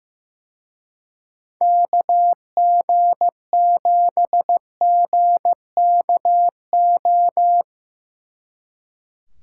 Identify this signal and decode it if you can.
Morse code, heard as a keyed tone, about 17 wpm, 700 Hz. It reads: KG7GKO
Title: Speed Racing - USA Call Signs 15wpm